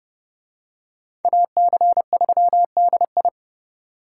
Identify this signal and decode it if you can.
Morse code, heard as a keyed tone, about 28 wpm, 700 Hz. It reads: AC3DI